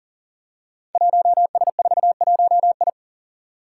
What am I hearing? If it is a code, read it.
Morse code, heard as a keyed tone, about 42 wpm, 700 Hz. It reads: 1S41I
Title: Five Challenging Letter-Number Combinations 40wpm